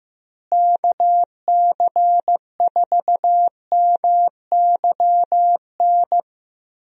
Morse code, heard as a keyed tone, about 16 wpm, 700 Hz. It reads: KC4MYN
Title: USA Call Signs - 6 Characters - Repeated 2x - 15wpm